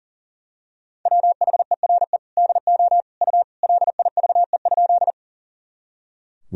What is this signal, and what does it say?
Morse code, heard as a keyed tone, about 40 wpm, 700 Hz. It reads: WHERE DO U LIVE?
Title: Sentences from Top 200 Words and Abbreviations 40wpm